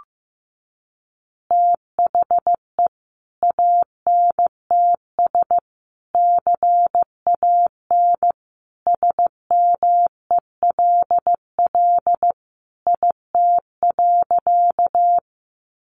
Morse code, heard as a keyed tone, about 15 wpm, 700 Hz. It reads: THE ANTS CAN SMELL IT.
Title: Additional Easy Phrases 15wpm